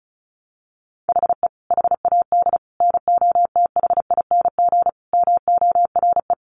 Morse code, heard as a keyed tone, about 34 wpm, 700 Hz. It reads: HE HAD NOTHING MORE
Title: Sentences from Top 1100 Words 35wpm